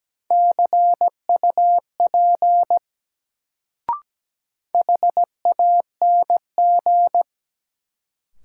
Morse code, heard as a keyed tone, about 17 wpm, 700 Hz. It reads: CUP HANG
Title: Sentences from Top 1600 Words - New Word Drill - 17wpm